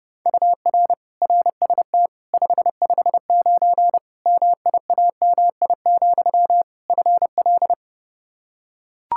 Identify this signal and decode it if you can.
Morse code, heard as a keyed tone, about 30 wpm, 700 Hz. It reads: UR RST 559 MIAMI, FL
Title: K1USN SST QSO Practice 30wpm